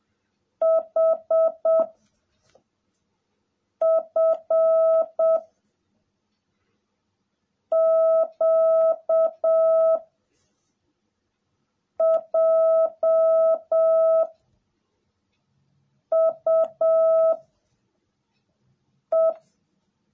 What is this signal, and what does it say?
Morse code, heard as a keyed tone, about 7 wpm, 660 Hz. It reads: H F Q J U E
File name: jh8cbh / 2021cw / cw18a